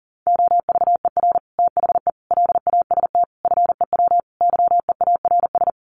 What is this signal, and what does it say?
Morse code, heard as a keyed tone, about 40 wpm, 700 Hz. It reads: OVER THE LAST FEW YEARS
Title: Sets of 5 Words Encore 40wpm